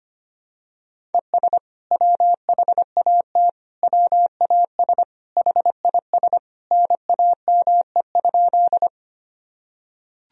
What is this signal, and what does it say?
Morse code, heard as a keyed tone, about 25 wpm, 700 Hz. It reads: ES WHAT WAS HIS NAME?
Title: Sentences from Top 200 Words and Abbreviations 25wpm